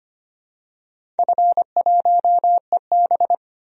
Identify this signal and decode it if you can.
Morse code, heard as a keyed tone, about 25 wpm, 700 Hz. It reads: F1EB
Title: DXCC-100 Call Signs - 4 Characters - 25wpm